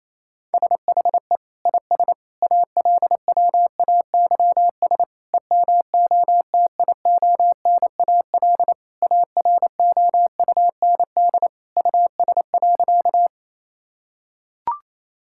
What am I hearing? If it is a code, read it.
Morse code, heard as a keyed tone, about 28 wpm, 700 Hz. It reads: SHE IS ALWAYS EMOTIONAL AROUND US.
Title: Sentences from Top 1000 Words - Review Part 5 - 28wpm